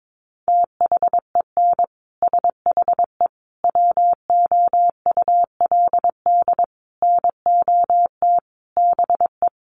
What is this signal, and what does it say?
Morse code, heard as a keyed tone, about 22 wpm, 700 Hz. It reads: THEN SHE WOULD NOT BE
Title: Sentences from Top 1400 Words 22wpm